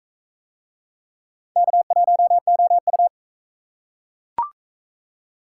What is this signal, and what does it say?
Morse code, heard as a keyed tone, about 37 wpm, 700 Hz. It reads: K1OU